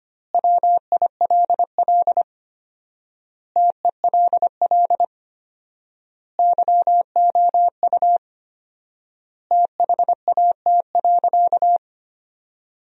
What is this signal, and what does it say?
Morse code, heard as a keyed tone, about 25 wpm, 700 Hz. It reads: WILL TELL YOU THAT.
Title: 4x Word-Spacing - Sentences from Top 300 Words - 25wpm